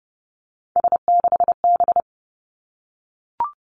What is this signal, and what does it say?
Morse code, heard as a keyed tone, about 30 wpm, 700 Hz. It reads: S6B
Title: Three Challenging Letter-Number Combinations 30wpm